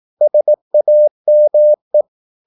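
Morse code, heard as a keyed tone, about 18 wpm, 600 Hz. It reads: SAME
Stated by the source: Morse practice at 18 wpm, English text, from CW med Gnister.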